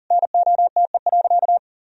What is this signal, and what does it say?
Morse code, heard as a keyed tone, about 40 wpm, 700 Hz. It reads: NOTE.
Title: Sentences from Top 1000 Words - Review Part 2 - 40wpm